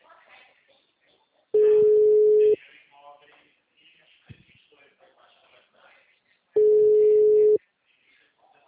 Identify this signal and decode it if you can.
Ringback tone (Europe, 425 Hz), with speech.